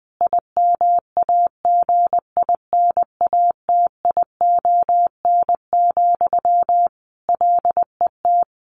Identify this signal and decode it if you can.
Morse code, heard as a keyed tone, about 20 wpm, 700 Hz. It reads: IMAGINATION, LET
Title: Sentences from Top 1300 Words 20wpm